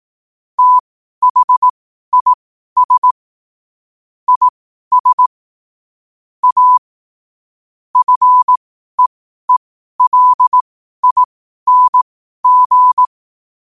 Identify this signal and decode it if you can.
Morse code, heard as a keyed tone, about 18 wpm, spaced as for 12 wpm, 1000 Hz. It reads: THIS IS A FEELING
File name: qotdfiles / 12wpm